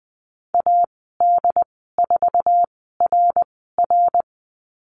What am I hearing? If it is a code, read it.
Morse code, heard as a keyed tone, about 20 wpm, spaced as for 14 wpm, 700 Hz. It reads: AD4RR
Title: 069 Lesson 12g - Letter d - Call Signs 20wpm